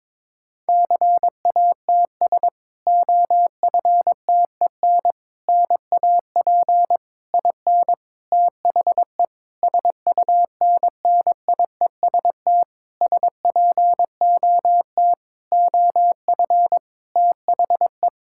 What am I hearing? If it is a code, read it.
Morse code, heard as a keyed tone, about 21 wpm, 700 Hz. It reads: CATS OFTEN NAP IN THE SUNNIEST SPOT OF THE
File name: Difficult Sentences with Hints - 22wpm